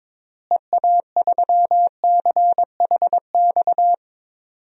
Morse code, heard as a keyed tone, about 22 wpm, 700 Hz. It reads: EA3CHX
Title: DXCC-100 Call Signs - 6 Characters - Repeated 2x - 22wpm